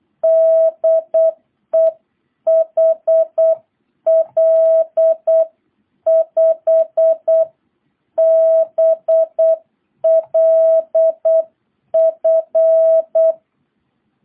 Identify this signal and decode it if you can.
Morse code, heard as a keyed tone, about 8 wpm, 660 Hz. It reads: DE HL5BLF